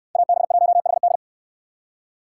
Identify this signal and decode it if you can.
Morse code, heard as a keyed tone, about 69 wpm, 700 Hz. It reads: AB1HD